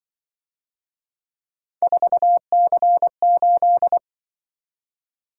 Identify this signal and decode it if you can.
Morse code, heard as a keyed tone, about 24 wpm, 700 Hz. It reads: 4C8
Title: Speed-Racing - Three Letter-Number Combinations - 20wpm